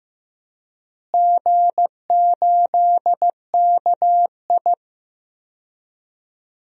Morse code, heard as a keyed tone, about 15 wpm, 700 Hz. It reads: G8KI
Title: DXCC-100 Call Signs - 4 Characters - Repeated 2x - 15wpm